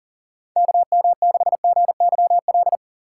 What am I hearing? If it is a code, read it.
Morse code, heard as a keyed tone, about 40 wpm, 700 Hz. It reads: KM6GYL
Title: USA Call Signs - 6 Characters - 4x Character Spacing - 40wpm